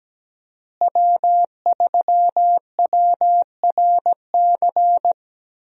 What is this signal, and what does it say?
Morse code, heard as a keyed tone, about 17 wpm, 700 Hz. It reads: W3WRC
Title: USA Call Signs - 5 Characters - Repeated 2x - 17wpm